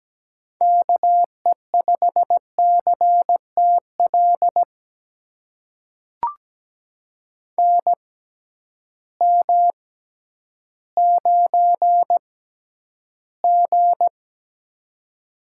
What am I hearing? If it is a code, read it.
Morse code, heard as a keyed tone, about 17 wpm, spaced as for 10 wpm, 700 Hz. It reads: KE5CTL NM9G
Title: USA Call Signs - 6x Character Spacing - 17wpm